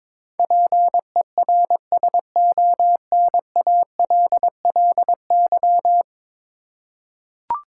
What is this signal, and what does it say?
Morse code, heard as a keyed tone, about 22 wpm, 700 Hz. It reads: PERSONALLY
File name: practice / 22wpm / 10 Letter Words - First 3 Letters Given - 22wpm